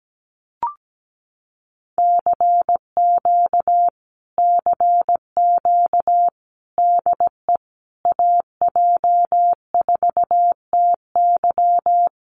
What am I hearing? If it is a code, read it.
Morse code, heard as a keyed tone, about 17 wpm, 700 Hz. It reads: CQ CQ DE AJ4TY
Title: Common QSO Phrases Encore 17wpm